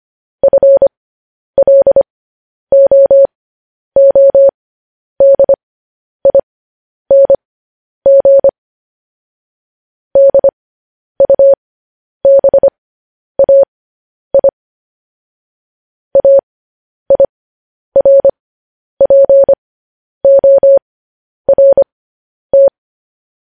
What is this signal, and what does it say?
Morse code, heard as a keyed tone, about 25 wpm, spaced as for 10 wpm, 550 Hz. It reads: FLOODING DUBAI AIRPORT
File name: podcast / News headlines-1x-25x10